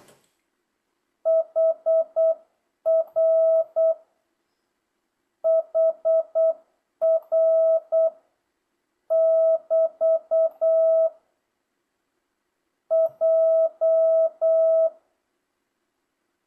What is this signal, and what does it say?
Morse code, heard as a keyed tone, about 8 wpm, 650 Hz. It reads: HR HR <BT> J
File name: jh8cbh / 2025cw